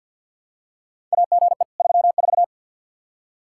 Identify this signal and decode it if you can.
Morse code, heard as a keyed tone, about 50 wpm, 700 Hz. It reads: AGE 34
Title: K1USN SST QSO Practice 50wpm